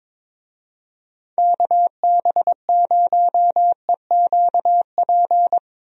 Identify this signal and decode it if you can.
Morse code, heard as a keyed tone, about 22 wpm, 700 Hz. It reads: KB0EQP
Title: USA Call Signs - 6 Characters - Repeated 2x - 22wpm